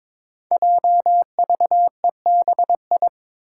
Morse code, heard as a keyed tone, about 22 wpm, 700 Hz. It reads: JVEBI
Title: Five Challenging Letter-Number Combinations 22wpm